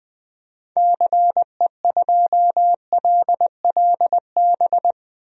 Morse code, heard as a keyed tone, about 20 wpm, 700 Hz. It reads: CE2LLB
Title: DXCC-100 Call Signs - 6 Characters - 20wpm